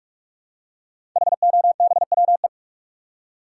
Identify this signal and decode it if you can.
Morse code, heard as a keyed tone, about 44 wpm, 700 Hz. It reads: HOBWE
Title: Code Groups - 5 Letters - 45wpm